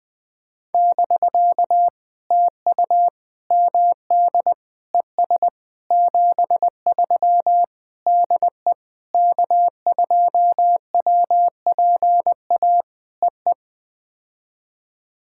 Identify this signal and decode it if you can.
Morse code, heard as a keyed tone, about 19 wpm, 700 Hz. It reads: <BK> TU MD ES 73 DE K2WPA EE
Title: POTA QSO Practice - 20wpm